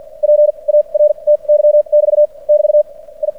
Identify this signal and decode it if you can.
Morse code, heard as a keyed tone, about 56 wpm, spaced as for 36 wpm, 600 Hz. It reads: PARTY <BT> <BT> I
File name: archived / CRSnet